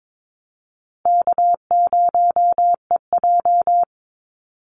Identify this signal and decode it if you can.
Morse code, heard as a keyed tone, about 22 wpm, 700 Hz. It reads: K0EJ
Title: Call Signs Repeated 2x - 22wpm